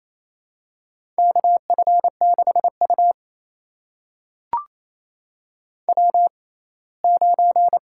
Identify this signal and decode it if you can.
Morse code, heard as a keyed tone, about 28 wpm, spaced as for 19 wpm, 700 Hz. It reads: KF6U W9
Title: USA Call Signs - 4 Characters - 6x Character Spacing - 28wpm